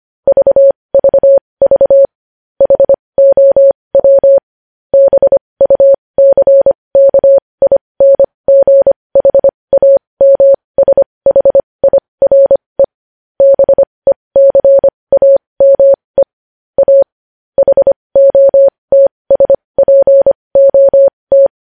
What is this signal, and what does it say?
Morse code, heard as a keyed tone, about 25 wpm, spaced as for 20 wpm, 550 Hz. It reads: VVV HOW BUCKINGHAMSHIRE BECAME A HOTSPOT
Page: news-headlines-in-morse-code-at-20-wpm